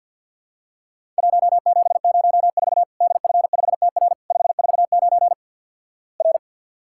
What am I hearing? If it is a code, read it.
Morse code, heard as a keyed tone, about 50 wpm, 700 Hz. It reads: 1704 DF5TR 549 R
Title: Worked All Europe DX Contest - QTC Exchange - 50wpm